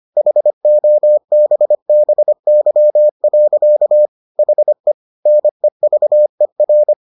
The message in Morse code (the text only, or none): HOBBY. HE NEVER